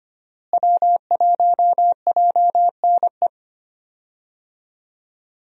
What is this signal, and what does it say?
Morse code, heard as a keyed tone, about 25 wpm, 700 Hz. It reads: W1JNE
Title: USA Call Signs - 5 Characters - Repeated 3x - 25wpm